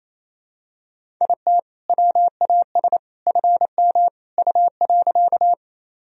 Morse code, heard as a keyed tone, about 28 wpm, 700 Hz. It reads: IT WAS FM U.
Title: Sentences from Top 100 Words and Abbreviations 28wpm